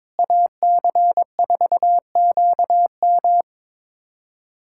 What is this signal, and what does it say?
Morse code, heard as a keyed tone, about 22 wpm, 700 Hz. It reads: AC4QM
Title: USA Callsign DE Callsign - 5 Characters - 22wpm